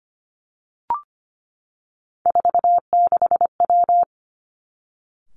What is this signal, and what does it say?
Morse code, heard as a keyed tone, about 25 wpm, 700 Hz. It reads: 46W
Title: Three Challenging Letter-Number Combinations 25wpm